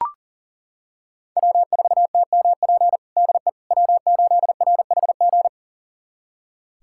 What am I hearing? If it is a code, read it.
Morse code, heard as a keyed tone, about 37 wpm, 700 Hz. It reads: W4TMP DE W8RHG
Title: USA Callsign DE Callsign - 5 Characters - 40wpm